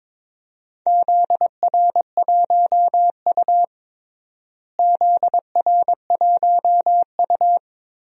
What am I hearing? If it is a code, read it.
Morse code, heard as a keyed tone, about 22 wpm, 700 Hz. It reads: ZR1U ZR1U
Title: DXCC-100 Call Signs - 4 Characters - Repeated 2x - 22wpm